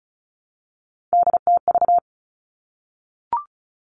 Morse code, heard as a keyed tone, about 36 wpm, 700 Hz. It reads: DTV
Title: Three Challenging Letter-Number Combinations 35wpm